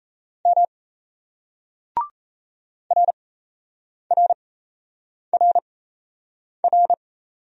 Morse code, heard as a keyed tone, about 33 wpm, spaced as for 8 wpm, 700 Hz. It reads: M RRRR
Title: Speed-Racing Single Letters 28wpm